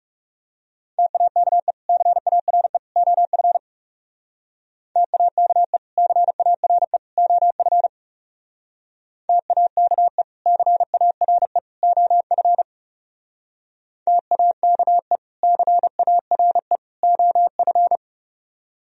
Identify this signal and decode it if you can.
Morse code, heard as a keyed tone, about 37 wpm, 700 Hz. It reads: TAKE CARE OF TAKE CARE OF TAKE CARE OF TAKE CARE OF
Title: Speed-Racing - Sets of 3 Words 30wpm